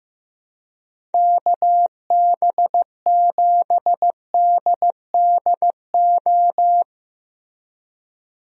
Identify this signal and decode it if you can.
Morse code, heard as a keyed tone, about 16 wpm, 700 Hz. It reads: KB7DDO
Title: USA Call Signs - 6 Characters - Repeated 2x - 15wpm